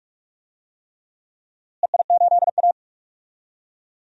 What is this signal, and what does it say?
Morse code, heard as a keyed tone, about 46 wpm, 700 Hz. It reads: EI8A
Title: DXCC-100 Call Signs - 4 Characters - Repeated 3x - 45wpm